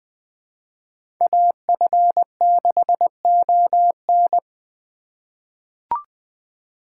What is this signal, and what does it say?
Morse code, heard as a keyed tone, about 20 wpm, 700 Hz. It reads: AF6ON
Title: USA Call Signs - 5 Characters - 2x Character Spacing - 20wpm